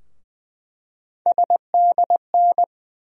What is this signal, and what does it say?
Morse code, heard as a keyed tone, about 20 wpm, 700 Hz. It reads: SDN
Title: USA Call Sign Postfixes - 3 Letters 20wpm